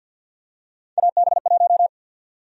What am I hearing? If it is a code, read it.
Morse code, heard as a keyed tone, about 50 wpm, 700 Hz. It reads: AB1
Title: USA Call Sign Prefixes - 50wpm